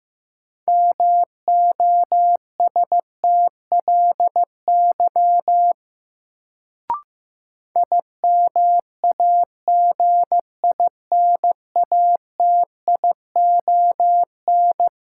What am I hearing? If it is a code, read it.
Morse code, heard as a keyed tone, about 15 wpm, 700 Hz. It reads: MOSTLY IMAGINATION
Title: Sentences from Top 1300 Words - New Word Drill - 15wpm